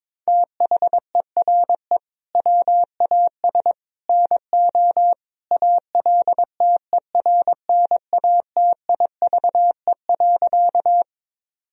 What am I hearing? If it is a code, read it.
Morse code, heard as a keyed tone, about 22 wpm, 700 Hz. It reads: THERE WAS NO ALTERNATIVE.
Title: Sentences from Top 1000 Words - Review Part 4 - 22wpm